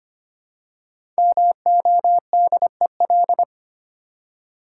Morse code, heard as a keyed tone, about 25 wpm, 700 Hz. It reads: MODEL
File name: 5 Letter Words - First 3 Letters Given - 25wpm